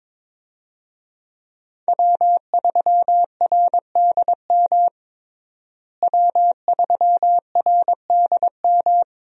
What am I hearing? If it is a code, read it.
Morse code, heard as a keyed tone, about 22 wpm, 700 Hz. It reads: W3RDM W3RDM
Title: USA Call Signs - 5 Characters - Repeated 2x - 22wpm